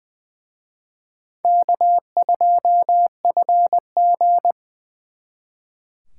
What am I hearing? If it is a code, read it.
Morse code, heard as a keyed tone, about 24 wpm, 700 Hz. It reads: K2FG
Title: Speed Racing - USA 1x2 Call Signs 20wpm